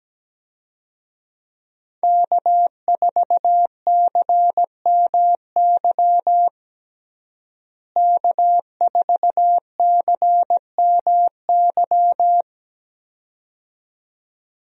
Morse code, heard as a keyed tone, about 17 wpm, 700 Hz. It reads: K4CMY K4CMY
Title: USA Call Signs - 5 Characters - Repeated 2x - 17wpm